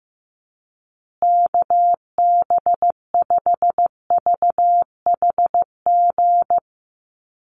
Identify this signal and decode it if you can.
Morse code, heard as a keyed tone, about 15 wpm, 700 Hz. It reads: KB5VHG
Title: USA Call Signs Repeated 3x 15wpm